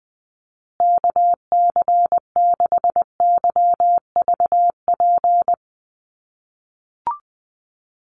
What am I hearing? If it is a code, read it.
Morse code, heard as a keyed tone, about 20 wpm, 700 Hz. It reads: KC6YVP
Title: USA Call Signs Repeated 3x 20wpm